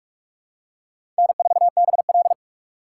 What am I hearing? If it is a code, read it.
Morse code, heard as a keyed tone, about 47 wpm, 700 Hz. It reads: N4BL
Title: USA Call Signs - 4 Characters - 4x Character Spacing - 45wpm